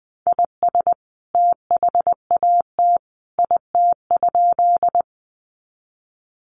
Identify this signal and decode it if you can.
Morse code, heard as a keyed tone, about 20 wpm, 700 Hz. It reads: IS THAT IT?
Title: Sentences from Top 100 Words 20wpm